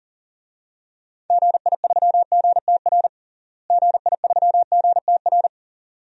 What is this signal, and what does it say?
Morse code, heard as a keyed tone, about 40 wpm, 700 Hz. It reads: GI3GTR GI3GTR
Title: DXCC-100 Call Signs - 6 Characters - Repeated 2x - 40wpm